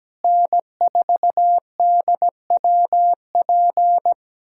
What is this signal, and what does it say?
Morse code, heard as a keyed tone, about 17 wpm, 700 Hz. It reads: N4DWP
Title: USA Callsign DE Callsign 17wpm